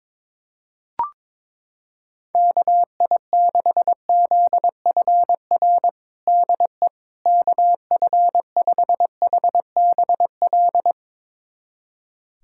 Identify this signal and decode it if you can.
Morse code, heard as a keyed tone, about 22 wpm, 700 Hz. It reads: KI6ZFR DE KF5HBL
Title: USA Callsign DE Callsign - 6 Characters - 22wpm